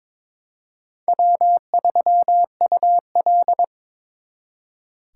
Morse code, heard as a keyed tone, about 26 wpm, 700 Hz. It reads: W3UL
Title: Speed Racing - USA Call Signs 22wpm